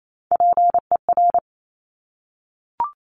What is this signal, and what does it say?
Morse code, heard as a keyed tone, about 28 wpm, 700 Hz. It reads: PER
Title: Top Three Letter Combinations in Words 28wpm